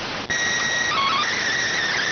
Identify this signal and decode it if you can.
SSTV (slow-scan television), PD 290.